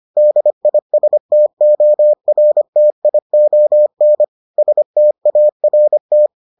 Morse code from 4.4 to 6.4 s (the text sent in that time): START